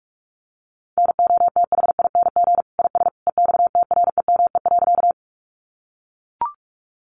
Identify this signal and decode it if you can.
Morse code, heard as a keyed tone, about 44 wpm, 700 Hz. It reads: NOTHING IS EXTREME.
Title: Sentences from Top 1300 Words 45wpm